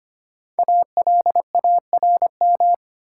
Morse code, heard as a keyed tone, about 25 wpm, 700 Hz. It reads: ALARM
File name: 5 Letter Words - First 3 Letters Given - 25wpm